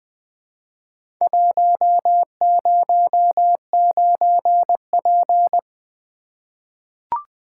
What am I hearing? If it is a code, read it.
Morse code, heard as a keyed tone, about 20 wpm, 700 Hz. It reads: 109P
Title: Four Letter-Number Combinations - 20wpm